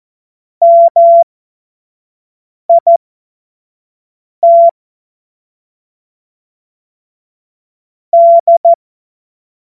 Morse code, heard as a keyed tone, about 14 wpm, spaced as for 5 wpm, 680 Hz. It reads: MIT D
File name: Morsekurs / Witze / Witz 30 Januar 2024 25 F70